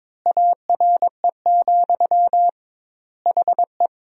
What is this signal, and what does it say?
Morse code, heard as a keyed tone, about 22 wpm, 700 Hz. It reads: ARE, HE
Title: 2x Word-Spacing - Sentences from Top 100 Words - 22wpm